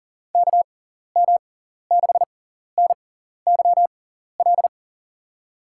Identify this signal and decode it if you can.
Morse code, heard as a keyed tone, about 37 wpm, spaced as for 14 wpm, 700 Hz. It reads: KM6NYL
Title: USA Call Signs - 6 Characters - 6x Character Spacing - 40wpm